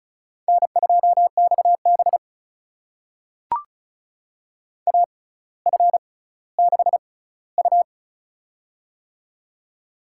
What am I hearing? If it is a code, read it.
Morse code, heard as a keyed tone, about 36 wpm, spaced as for 18 wpm, 700 Hz. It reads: N2XB AF6U
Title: USA Call Signs - 6x Character Spacing - 35wpm